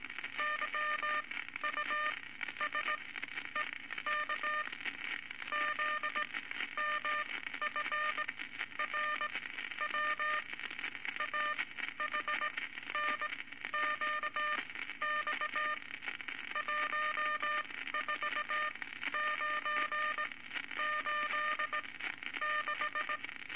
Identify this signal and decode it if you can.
Morse code, heard as a keyed tone, about 18 wpm, spaced as for 12 wpm, 1210 Hz, with noise. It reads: YUSEK ZMFRW AHNQX 14986
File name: Morse Code Records / 19 A LCM 166 Ten Words Per Minute